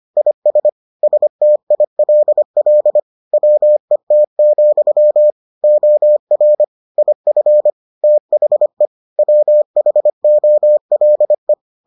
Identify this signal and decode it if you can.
Morse code, heard as a keyed tone, about 25 wpm, 600 Hz. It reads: IS STILL WET, OR IF THE WHOLE